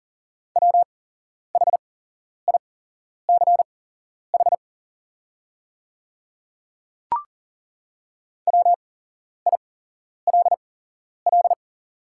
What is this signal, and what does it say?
Morse code, heard as a keyed tone, about 40 wpm, spaced as for 11 wpm, 700 Hz. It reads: WHICH WILL
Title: 8x Character Spacing - Top 100 Words - 40wpm